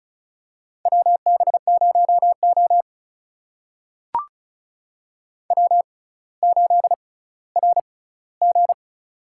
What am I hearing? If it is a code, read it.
Morse code, heard as a keyed tone, about 36 wpm, spaced as for 18 wpm, 700 Hz. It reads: WB0O W8RG
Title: USA Call Signs - 4 Characters - 6x Character Spacing - 35wpm